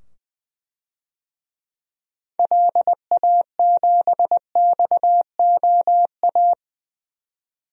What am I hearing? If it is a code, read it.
Morse code, heard as a keyed tone, about 20 wpm, 700 Hz. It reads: LA7XOA
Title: DXCC-100 Call Signs - 6 Characters - 20wpm